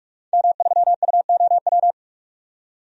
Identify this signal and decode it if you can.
Morse code, heard as a keyed tone, about 44 wpm, 700 Hz. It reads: M3UOW
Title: DXCC-100 Call Signs - 5 Characters - 45wpm